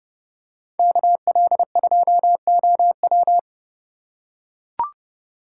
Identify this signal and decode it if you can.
Morse code, heard as a keyed tone, about 30 wpm, 700 Hz. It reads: KL2OW